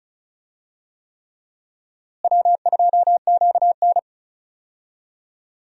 Morse code, heard as a keyed tone, about 36 wpm, 700 Hz. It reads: W2QN
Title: USA Call Signs - 4 Characters - Repeated 3x - 35wpm